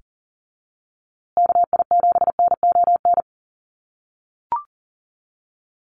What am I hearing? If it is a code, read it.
Morse code, heard as a keyed tone, about 40 wpm, 700 Hz. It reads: KI7NON
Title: USA Call Signs Repeated 3x 40wpm